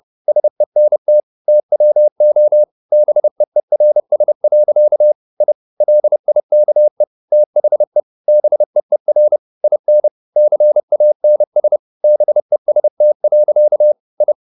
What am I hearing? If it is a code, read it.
Morse code, heard as a keyed tone, about 30 wpm, 600 Hz. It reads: SENT TWO BEERS. I LIKE THE BEER IN CANS BEST. I
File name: the old man and the sea / the old man and the sea 1 30wpm